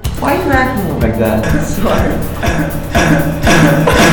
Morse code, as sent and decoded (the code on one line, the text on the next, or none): none